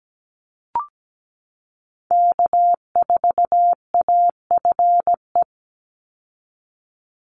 Morse code, heard as a keyed tone, about 17 wpm, 700 Hz. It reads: K4AFE
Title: Call Signs Repeated 2x - 17wpm